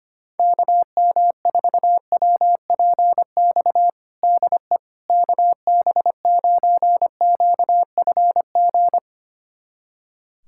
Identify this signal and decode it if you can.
Morse code, heard as a keyed tone, about 25 wpm, 700 Hz. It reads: KM4WPX DE KB9QFG
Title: USA Callsign DE Callsign - 6 Characters - 25wpm